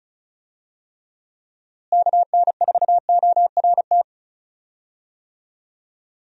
Morse code, heard as a keyed tone, about 36 wpm, 700 Hz. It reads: KN4ORT
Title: USA Call Signs - 6 Characters - Repeated 2x - 35wpm